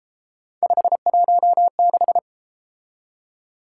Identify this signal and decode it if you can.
Morse code, heard as a keyed tone, about 32 wpm, 700 Hz. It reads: 516